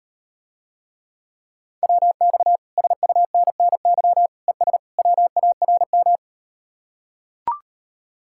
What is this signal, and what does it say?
Morse code, heard as a keyed tone, about 38 wpm, 700 Hz. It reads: WX SUNNY ES WARM